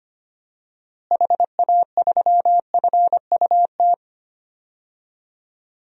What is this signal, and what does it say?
Morse code, heard as a keyed tone, about 25 wpm, 700 Hz. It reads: HA3FUT
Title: DXCC-100 Call Signs - 6 Characters - Repeated 2x - 25wpm